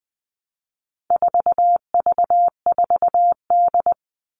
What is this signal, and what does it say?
Morse code, heard as a keyed tone, about 20 wpm, 700 Hz. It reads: 4V4D
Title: All Cut Numbers - 4-Digit 20wpm